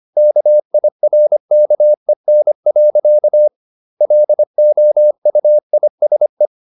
Morse code, heard as a keyed tone, about 25 wpm, 600 Hz. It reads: KIRKEN. LOUISE
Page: Ferie 25 wpm